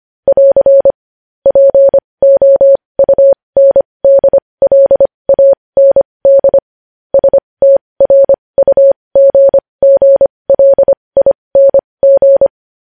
Morse code, heard as a keyed tone, about 25 wpm, spaced as for 20 wpm, 550 Hz. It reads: <AR> POUNDLAND STRUGGLING
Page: News headlines in Morse code at 20 wpm